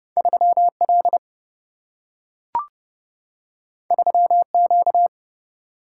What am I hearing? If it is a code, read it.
Morse code, heard as a keyed tone, about 30 wpm, 700 Hz. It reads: 3L 3Q
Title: Single-Letter Single-Number 30wpm